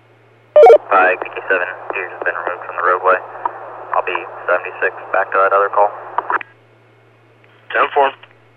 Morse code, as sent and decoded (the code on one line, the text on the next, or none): none